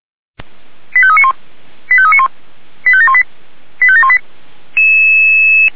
Five-tone selective call (ZVEI), heard as quick runs of five tones, tones of about 70 ms. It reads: 85391 85391 85718 85718